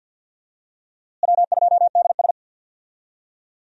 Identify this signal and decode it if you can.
Morse code, heard as a keyed tone, about 50 wpm, 700 Hz. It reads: W2DS
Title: USA Call Signs - 4 Characters - Repeated 3x - 50wpm